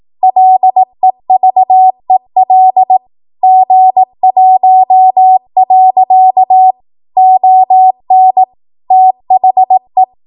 Morse code, heard as a keyed tone, about 18 wpm, 750 Hz. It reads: LEVEL G1. ON THE
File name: Morse / 18WPM